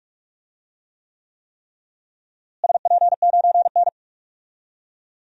Morse code, heard as a keyed tone, about 46 wpm, 700 Hz. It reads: SP9N
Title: DXCC-100 Call Signs - 4 Characters - Repeated 3x - 45wpm